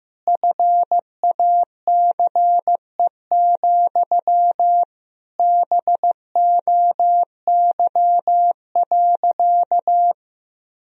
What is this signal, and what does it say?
Morse code, heard as a keyed tone, about 15 wpm, 700 Hz. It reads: FACE, BOY.
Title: Sentences from Top 1000 Words - Review Part 1 - 15wpm